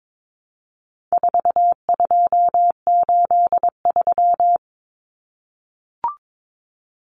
Morse code, heard as a keyed tone, about 22 wpm, 700 Hz. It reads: 4283
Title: Common Cut Numbers - 4-Digit 22wpm